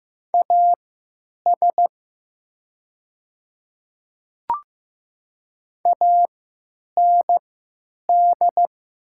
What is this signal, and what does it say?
Morse code, heard as a keyed tone, about 15 wpm, spaced as for 9 wpm, 690 Hz. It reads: AS AND
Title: 3x Character Spacing - Top 100 Words - 15wpm